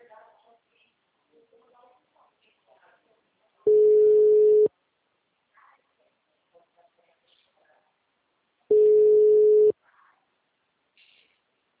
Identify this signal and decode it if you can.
Ringback tone (Europe, 425 Hz), with speech.